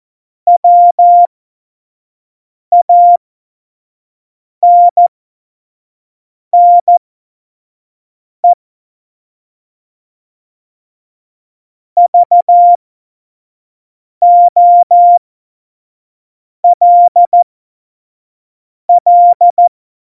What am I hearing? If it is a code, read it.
Morse code, heard as a keyed tone, about 14 wpm, spaced as for 5 wpm, 700 Hz. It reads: WANNE VOLL
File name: Morsekurs / Witze / Witz 30. Januar 2018, 25 F70